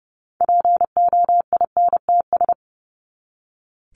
Morse code, heard as a keyed tone, about 30 wpm, 700 Hz. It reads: POINTS
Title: ES:S Suffix - Top 100 Words with Base Words 30wpm